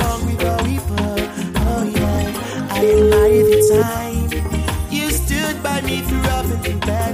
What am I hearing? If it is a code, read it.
Ringback tone (Europe, 425 Hz), with noise.